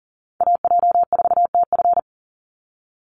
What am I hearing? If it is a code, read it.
Morse code, heard as a keyed tone, about 40 wpm, 700 Hz. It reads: AJ4TF
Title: USA Call Signs 40wpm